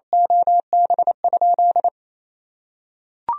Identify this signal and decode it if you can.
Morse code, heard as a keyed tone, about 28 wpm, 700 Hz. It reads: OB?
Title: Sentences from Top 1000 Words - Review Part 3 - 28wpm